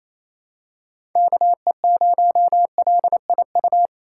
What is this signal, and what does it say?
Morse code, heard as a keyed tone, about 28 wpm, 700 Hz. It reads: KE0LIU
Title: USA Call Signs - 6 Characters - 4x Character Spacing - 28wpm